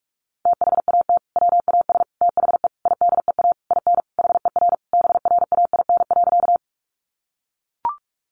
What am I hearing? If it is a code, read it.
Morse code, heard as a keyed tone, about 44 wpm, 700 Hz. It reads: THAT WAS THE IDEA IN HER BRAIN.
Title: Sentences from Top 1100 Words 45wpm